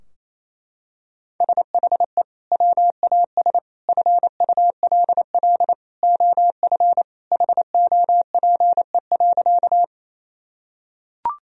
Morse code, heard as a keyed tone, about 28 wpm, 700 Hz. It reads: SHE WAS FULL OF HOPE.
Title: Sentences from Top 1000 Words - Review Part 3 - 28wpm